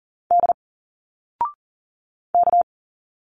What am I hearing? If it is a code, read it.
Morse code, heard as a keyed tone, about 40 wpm, 700 Hz. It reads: D K
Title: Single Letters 40wpm